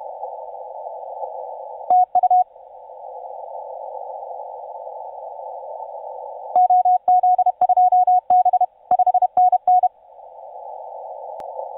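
Morse code, heard as a keyed tone, about 31 wpm, 730 Hz. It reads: TU OZ2B 5NN
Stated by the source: TU7C on 20 metre CW, DX station recording by IZ8GNR.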